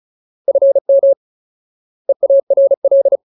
Morse code, heard as a keyed tone, about 34 wpm, 550 Hz. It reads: FM EARL
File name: Ultimate Word List / 35WPM Ultimate